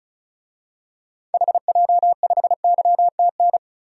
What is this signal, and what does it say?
Morse code, heard as a keyed tone, about 34 wpm, 700 Hz. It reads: HJ5YTN
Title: DXCC-100 Call Signs - 6 Characters - 35wpm